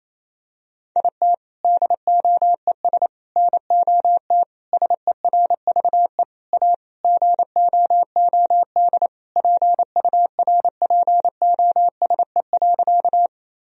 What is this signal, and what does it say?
Morse code, heard as a keyed tone, about 28 wpm, 700 Hz. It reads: IT DOES NOT SERVE A GOOD PURPOSE.
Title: Sentences from Top 1000 Words - Review Part 2 - 28wpm